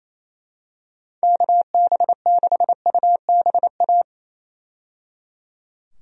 Morse code, heard as a keyed tone, about 28 wpm, 700 Hz. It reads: KB6UBA
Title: USA Call Signs - 6 Characters - 28wpm